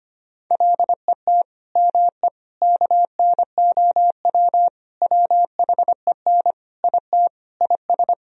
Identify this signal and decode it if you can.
Morse code, heard as a keyed tone, about 25 wpm, 700 Hz. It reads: LET ME KNOW WHEN IT IS
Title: Sentences from Top 1000 Words - Review Part 2 - 25wpm